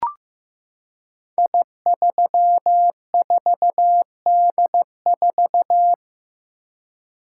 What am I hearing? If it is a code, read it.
Morse code, heard as a keyed tone, about 15 wpm, 700 Hz. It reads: I34D4
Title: Code Groups - 5 Letter:Numbers - 15wpm